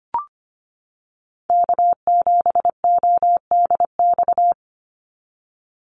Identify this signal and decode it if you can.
Morse code, heard as a keyed tone, about 25 wpm, 700 Hz. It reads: K7ODX
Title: NA SOTA Chaser Callsigns 25wpm